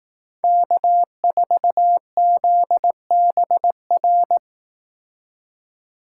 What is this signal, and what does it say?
Morse code, heard as a keyed tone, about 18 wpm, 700 Hz. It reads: K4ZBR
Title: Speed Racing - USA 1x3 Call Signs 15wpm